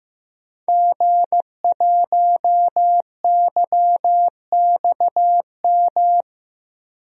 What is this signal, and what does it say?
Morse code, heard as a keyed tone, about 15 wpm, 700 Hz. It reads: G1YXM
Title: DXCC-100 Call Signs - 5 Characters - 15wpm